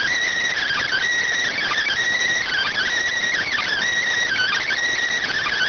Emergency Alert System (SAME) tones, heard as harsh buzzing bursts. It is heard from 0.5 to 4.7 s.
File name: findings / S ss-portrait.gen-19 rx